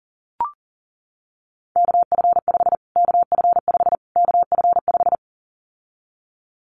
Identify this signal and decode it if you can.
Morse code, heard as a keyed tone, about 40 wpm, 700 Hz. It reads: KF5 KF5 KF5
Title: Callsign Prefixes for Top 25 DXCC - Repeated 3x - 40wpm